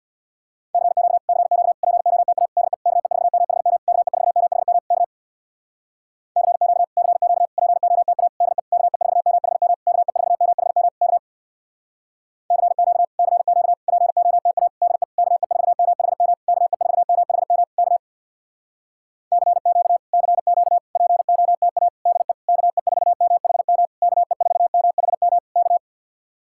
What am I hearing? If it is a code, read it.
Morse code, heard as a keyed tone, about 65 wpm, 700 Hz. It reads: CQ CQ POTA DE KE4MHM KE4MHM K CQ CQ POTA DE KE4MHM KE4MHM K CQ CQ POTA DE KE4MHM KE4MHM K CQ CQ POTA DE KE4MHM KE4MHM K